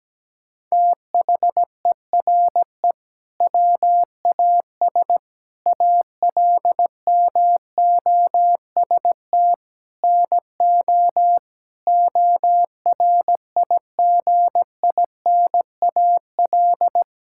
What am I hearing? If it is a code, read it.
Morse code, heard as a keyed tone, about 17 wpm, 700 Hz. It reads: THERE WAS ALMOST NO ORIGINAL
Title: Sentences from Top 1000 Words - Review Part 6 - 17wpm